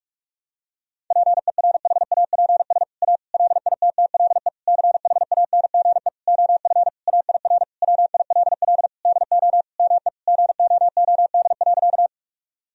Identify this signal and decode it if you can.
Morse code, heard as a keyed tone, about 42 wpm, 700 Hz. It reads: PERHAPS A LITTLE CHANGE OF AIR WILL DO ME GOOD.
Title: Sentences from Top 1000 Words - Review Part 1 - 45wpm